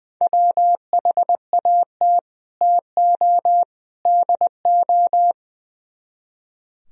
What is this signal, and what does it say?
Morse code, heard as a keyed tone, about 22 wpm, 700 Hz. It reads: WHAT TO DO
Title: Speed-Racing - Sets of 3 Words 20wpm